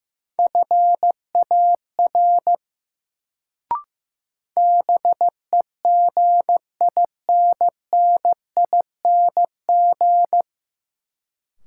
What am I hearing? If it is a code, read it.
Morse code, heard as a keyed tone, about 15 wpm, 700 Hz. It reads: FAR BEGINNING
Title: Sets of 3 Words - New Word Drill - 15wpm